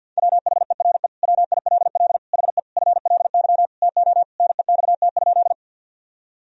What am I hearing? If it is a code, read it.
Morse code, heard as a keyed tone, about 50 wpm, 700 Hz. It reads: WHERE WILL HE FLY TO NEXT?
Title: Sentences from Top 1000 Words - Review Part 2 - 50wpm